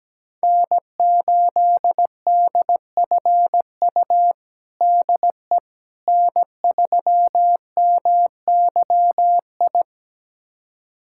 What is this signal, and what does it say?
Morse code, heard as a keyed tone, about 17 wpm, 700 Hz. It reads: N8DFU DE N3MYI
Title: USA Callsign DE Callsign - 5 Characters - 17wpm